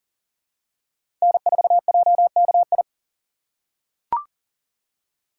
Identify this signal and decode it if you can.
Morse code, heard as a keyed tone, about 40 wpm, 700 Hz. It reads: N4JKI
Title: USA Call Signs - 5 Characters - 3x Character Spacing - 40wpm